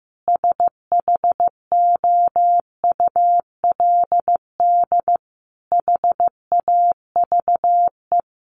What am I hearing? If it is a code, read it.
Morse code, heard as a keyed tone, about 14 wpm, 700 Hz. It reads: SHOULD HAVE
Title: Sentences from Top 1200 Words 15wpm